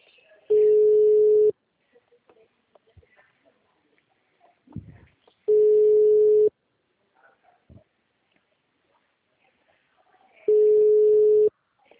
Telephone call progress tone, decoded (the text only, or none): ringback tone